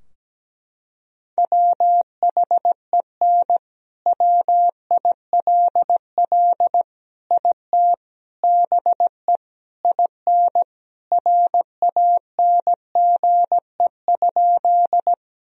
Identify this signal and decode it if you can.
Morse code, heard as a keyed tone, about 17 wpm, 700 Hz. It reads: WHEN WILL IT BE IN RANGE?
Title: Sentences from Top 1000 Words - Review Part 4 - 17wpm